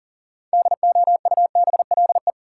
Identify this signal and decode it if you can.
Morse code, heard as a keyed tone, about 37 wpm, 700 Hz. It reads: DOUBLE
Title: Sentences from Top 1400 Words - New Word Drill Alt - 40wpm